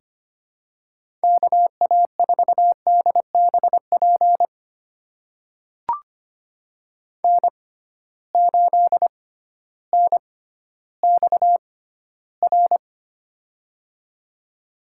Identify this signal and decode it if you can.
Morse code, heard as a keyed tone, about 25 wpm, spaced as for 13 wpm, 700 Hz. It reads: KA4DBP N8NXR
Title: USA Call Signs - 6x Character Spacing - 25wpm